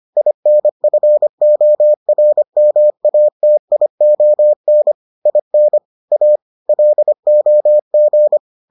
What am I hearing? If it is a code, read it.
Morse code, heard as a keyed tone, about 25 wpm, 600 Hz. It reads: INFORMATION IN A LOG